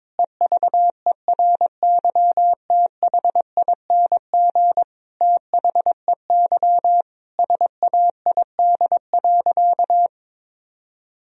Morse code, heard as a keyed tone, about 22 wpm, 700 Hz. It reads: EVERYTHING THEY SAID.
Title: Sentences from Top 1000 Words - Review Part 3 - 22wpm